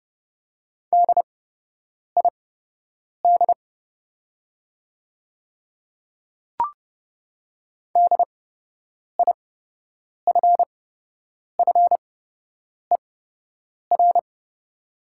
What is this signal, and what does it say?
Morse code, heard as a keyed tone, about 30 wpm, spaced as for 8 wpm, 700 Hz. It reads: DID DIFFER
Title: 8x Character Spacing - Top 200 Words - 30wpm